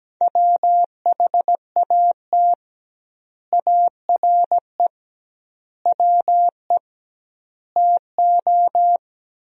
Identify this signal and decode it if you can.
Morse code, heard as a keyed tone, about 17 wpm, 700 Hz. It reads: WHAT ARE WE TO
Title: 2x Word-Spacing - Sentences from Top 100 Words - 17wpm